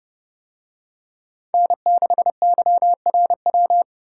Morse code, heard as a keyed tone, about 30 wpm, 700 Hz. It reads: N6YRW